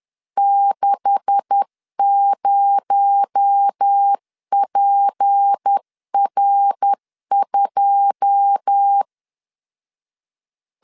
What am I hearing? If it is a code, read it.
Morse code, heard as a keyed tone, about 10 wpm, 800 Hz. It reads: 60PR2
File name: LEZIONI / 9 286 10wpm 2735